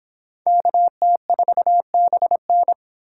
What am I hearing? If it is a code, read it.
Morse code, heard as a keyed tone, about 26 wpm, 700 Hz. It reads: KT4BN